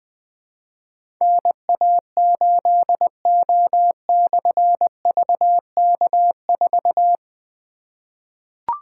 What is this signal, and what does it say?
Morse code, heard as a keyed tone, about 20 wpm, 700 Hz. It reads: NA8O/VK4
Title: Difficult Call Signs 20wpm